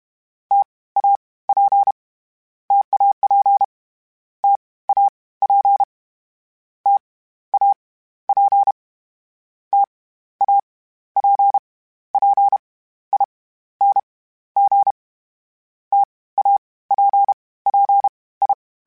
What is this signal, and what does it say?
Morse code, heard as a keyed tone, about 32 wpm, 800 Hz. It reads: T A P TAP T A P T A P T A P P I N G T A P P I
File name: Variable Speed / Variable Speed-mp3s / T AOEI's sp @30